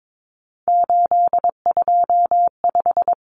Morse code, heard as a keyed tone, about 22 wpm, 700 Hz. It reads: 825
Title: Common Cut Numbers - 3-Digit 22wpm